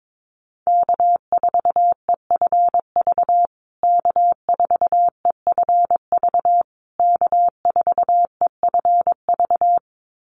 Morse code, heard as a keyed tone, about 22 wpm, 700 Hz. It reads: K4EFV K4EFV K4EFV
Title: USA Call Signs Repeated 3x 22wpm